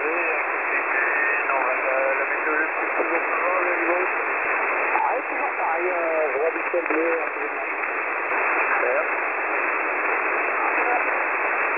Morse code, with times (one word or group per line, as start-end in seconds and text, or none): none